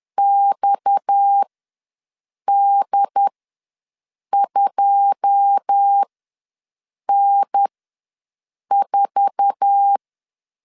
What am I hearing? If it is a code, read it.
Morse code, heard as a keyed tone, about 10 wpm, spaced as for 6 wpm, 800 Hz. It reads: XD2N4
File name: LEZIONI / 9 2 ws 10wpm 4130